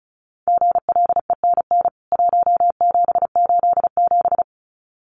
Morse code, heard as a keyed tone, about 34 wpm, 700 Hz. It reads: GLENN 1787
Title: CWT Exchange 35wpm